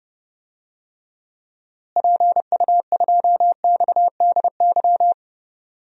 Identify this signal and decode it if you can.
Morse code, heard as a keyed tone, about 32 wpm, 700 Hz. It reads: PU2XDY
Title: DXCC-100 Call Signs - 6 Characters - Repeated 2x - 30wpm